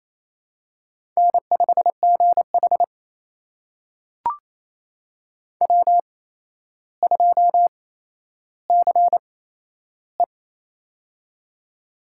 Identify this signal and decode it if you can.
Morse code, heard as a keyed tone, about 28 wpm, spaced as for 12 wpm, 700 Hz. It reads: N5GH W2CE
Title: USA Call Signs - 4 Characters - 8x Character Spacing - 28wpm